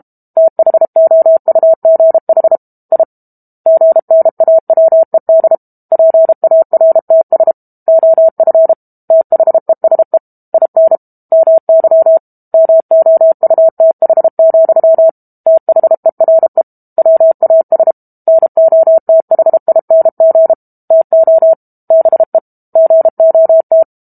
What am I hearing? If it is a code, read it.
Morse code, heard as a keyed tone, about 32 wpm, 650 Hz. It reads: THOUGH I GNAWED PARTS OF THESE IN MY MOUTH, THERE WAS NOTHING TO BE GOT
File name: Book Two / 06-Chapter 6 - 32 WPM